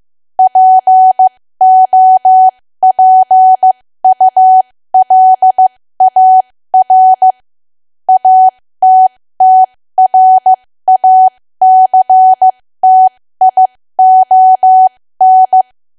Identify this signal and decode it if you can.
Morse code, heard as a keyed tone, about 15 wpm, spaced as for 13 wpm, 750 Hz. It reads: POPULAR ATTRACTION
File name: cwa / W1AW-Level2 / 13WPM